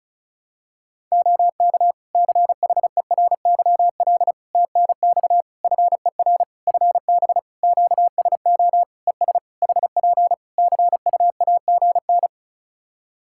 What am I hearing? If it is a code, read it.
Morse code, heard as a keyed tone, about 36 wpm, 700 Hz. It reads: OK CHERYL TNX FER FB QSO ES HP CUAGN
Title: Practice QSO - Exchange 4 - Ending - 35wpm